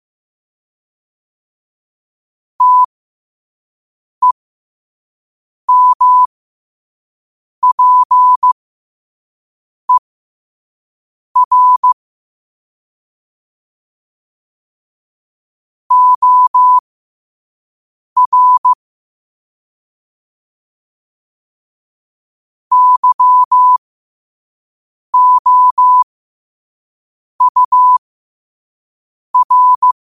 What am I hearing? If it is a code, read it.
Morse code, heard as a keyed tone, about 15 wpm, spaced as for 5 wpm, 1000 Hz. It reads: TEMPER OR YOUR
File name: qotdfiles / 5wpm